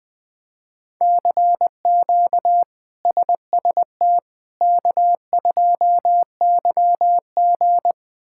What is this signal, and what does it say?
Morse code, heard as a keyed tone, about 20 wpm, 700 Hz. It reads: CQ SST K2YG
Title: K1USN SST QSO Practice - 20wpm